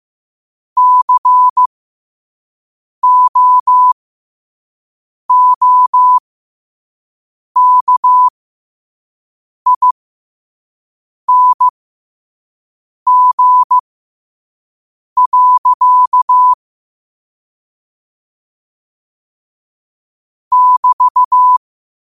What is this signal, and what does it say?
Morse code, heard as a keyed tone, about 15 wpm, spaced as for 5 wpm, 1000 Hz. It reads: COOKING. <BT>
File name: qotdfiles / 5wpm